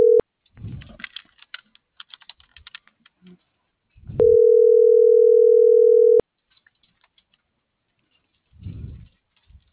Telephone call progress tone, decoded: ringback tone

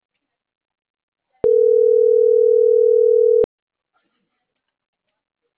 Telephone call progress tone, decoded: ringback tone

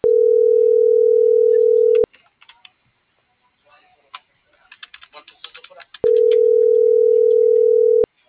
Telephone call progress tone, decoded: ringback tone

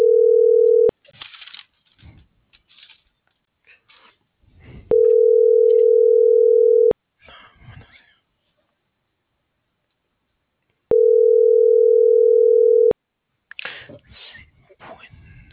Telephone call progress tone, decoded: ringback tone

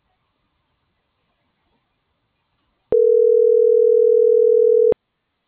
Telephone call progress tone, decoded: ringback tone

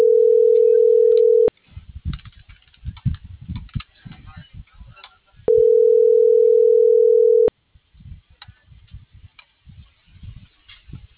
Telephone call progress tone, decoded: ringback tone

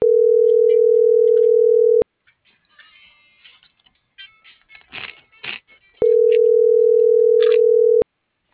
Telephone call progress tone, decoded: ringback tone

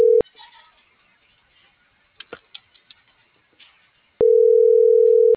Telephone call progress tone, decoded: ringback tone

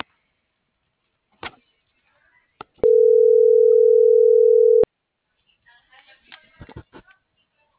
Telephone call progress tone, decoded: ringback tone

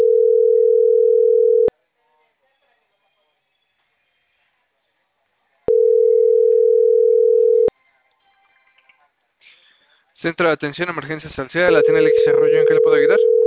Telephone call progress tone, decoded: ringback tone

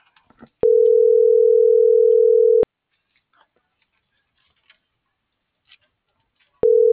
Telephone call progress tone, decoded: ringback tone

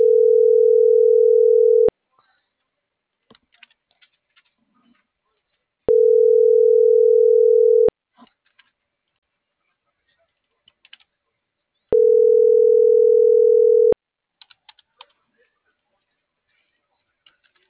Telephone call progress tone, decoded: ringback tone

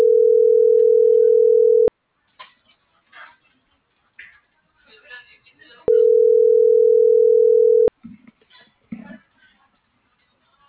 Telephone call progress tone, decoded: ringback tone

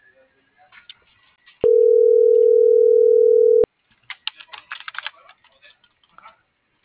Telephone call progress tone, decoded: ringback tone

